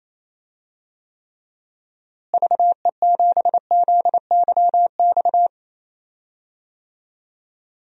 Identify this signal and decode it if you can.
Morse code, heard as a keyed tone, about 28 wpm, 700 Hz. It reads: VE7ZYX
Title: DXCC-100 Call Signs - 6 Characters - Repeated 2x - 28wpm